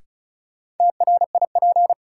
Morse code, heard as a keyed tone, about 36 wpm, 700 Hz. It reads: TRIP
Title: Sentences from Top 1000 Words - Review Part 4 - New Word Drill - 35wpm